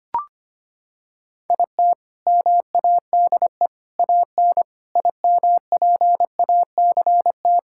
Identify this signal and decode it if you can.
Morse code, heard as a keyed tone, about 25 wpm, 700 Hz. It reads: IT MADE AN IMPACT
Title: Sentences from Top 1000 Words - Review Part 6 - 25wpm